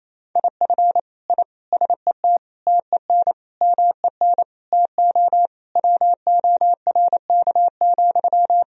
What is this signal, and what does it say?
Morse code, heard as a keyed tone, about 28 wpm, 700 Hz. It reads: IF I SET TEN MEN TO WORK,
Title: Sentences from Top 1000 Words - Review Part 2 - 28wpm